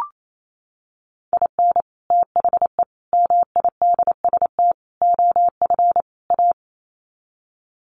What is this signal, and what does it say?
Morse code, heard as a keyed tone, about 28 wpm, 700 Hz. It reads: IN THE MIDST OF A
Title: Sets of 5 Words 28wpm